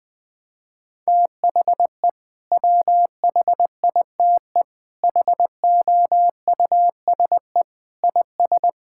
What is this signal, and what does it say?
Morse code, heard as a keyed tone, about 21 wpm, 700 Hz. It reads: THE WHITE HOUSE IS
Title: Sentences with 3-Word Phrases - 20wpm